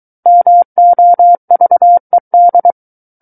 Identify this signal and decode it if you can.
Morse code, heard as a keyed tone, about 23 wpm, 700 Hz. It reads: MOVED
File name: Book One / 14-Chapter 14 - 23 WPM